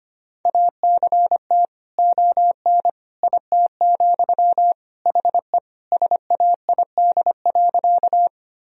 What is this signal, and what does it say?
Morse code, heard as a keyed tone, about 25 wpm, 700 Hz. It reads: ACT ON IT, HE SAID.
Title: Sentences from Top 1000 Words - Review Part 1 - 25wpm